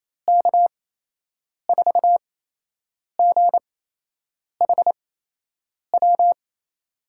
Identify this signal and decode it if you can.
Morse code, heard as a keyed tone, about 28 wpm, spaced as for 8 wpm, 700 Hz. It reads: K4GHW
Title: USA Call Signs - 6 Characters - 8x Character Spacing - 28wpm